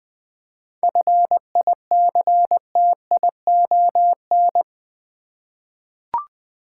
Morse code, heard as a keyed tone, about 20 wpm, 700 Hz. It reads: FICTION
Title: 7 Letter Words - First 3 Letters Given - 20wpm